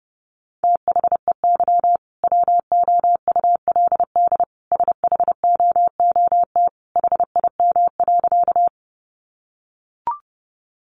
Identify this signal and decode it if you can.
Morse code, heard as a keyed tone, about 30 wpm, 700 Hz. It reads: THEY WOULD SHOOT HIM.
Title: Sentences from Top 1200 Words 30wpm